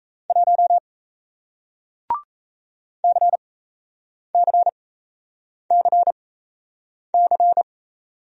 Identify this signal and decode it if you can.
Morse code, heard as a keyed tone, about 37 wpm, spaced as for 8 wpm, 700 Hz. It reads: 1 CCCC